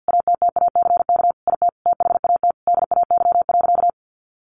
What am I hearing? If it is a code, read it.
Morse code, heard as a keyed tone, about 50 wpm, 700 Hz. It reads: ATTACK IT THAT DAY.
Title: Sentences from Top 1500 Words 50wpm